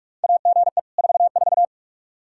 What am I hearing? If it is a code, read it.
Morse code, heard as a keyed tone, about 46 wpm, 700 Hz. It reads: AGE 44
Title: Practice QSO - Exchange 3 - More Info - 45wpm